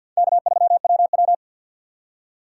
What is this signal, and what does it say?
Morse code, heard as a keyed tone, about 53 wpm, 700 Hz. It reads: K3WW
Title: USA Call Signs - 4 Characters - 3x Character Spacing - 50wpm